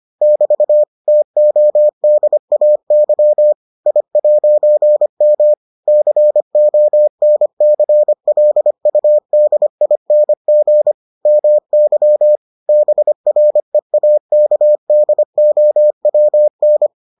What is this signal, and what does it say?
Morse code, heard as a keyed tone, about 25 wpm, 600 Hz. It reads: <BT> TODAY I'M CONCLUDING MY BREAKDOWN